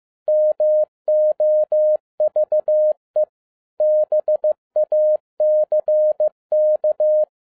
Morse code, heard as a keyed tone, about 15 wpm, 610 Hz. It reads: MOVE BACK
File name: 2011-07-mp3 / n07 A LOT HAS CHANGED 15wpm